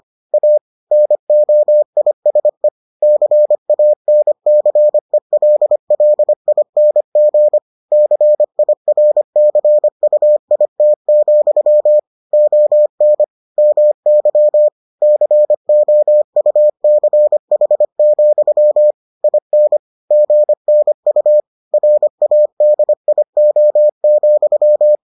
Morse code, heard as a keyed tone, about 25 wpm, 600 Hz. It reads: A NOISE CANCELLING CIRCUIT, ON MY COUCH, IN GNU RADIO,